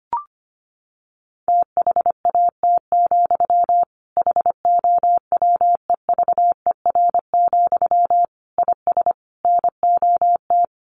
Morse code, heard as a keyed tone, about 25 wpm, 700 Hz. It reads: THAT, HOWEVER, IS NOT
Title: Sentences from Top 1500 Words 25wpm